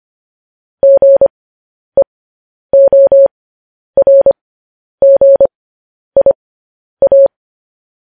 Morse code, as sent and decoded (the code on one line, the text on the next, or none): --. --- .-. --. .. .-
GEORGIA